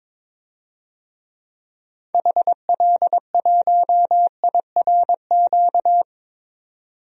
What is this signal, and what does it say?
Morse code, heard as a keyed tone, about 22 wpm, 700 Hz. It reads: HL1IRQ
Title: DXCC-100 Call Signs - 6 Characters - 22wpm